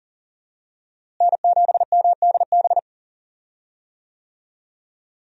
Morse code, heard as a keyed tone, about 40 wpm, 700 Hz. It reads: N7MDB
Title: USA Call Signs - 5 Characters - Repeated 3x - 40wpm